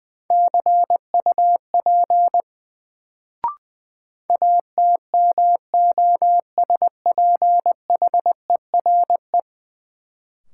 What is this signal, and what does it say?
Morse code, heard as a keyed tone, about 21 wpm, 700 Hz. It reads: CUP ATMOSPHERE
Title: Sentences from Top 1600 Words - New Word Drill - 20wpm